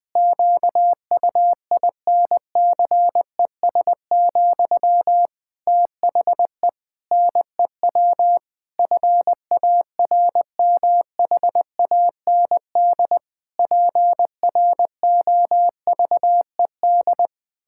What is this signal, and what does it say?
Morse code, heard as a keyed tone, about 20 wpm, 700 Hz. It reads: QUINCES, THE NEW FARMHAND PROVED
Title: Pangram Challenge v2 - 20wpm